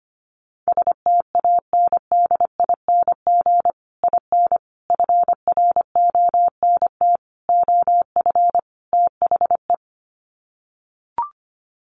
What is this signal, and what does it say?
Morse code, heard as a keyed tone, about 25 wpm, 700 Hz. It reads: STANDING IN FRONT OF THE
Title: Sets of 5 Words Encore2 25wpm